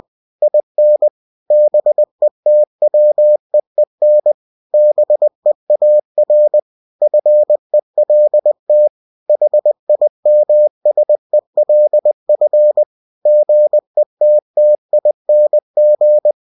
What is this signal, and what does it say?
Morse code, heard as a keyed tone, about 20 wpm, 600 Hz. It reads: IN BETWEEN BEAR FELT HIMSELF GETTING
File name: cw-books / winnie-the-pooh / winnie-20wpm-page-0019